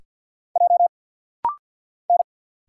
Morse code, heard as a keyed tone, about 50 wpm, 700 Hz. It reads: J N
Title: Single Letter-Number Alt - Rapid-Fire 50wpm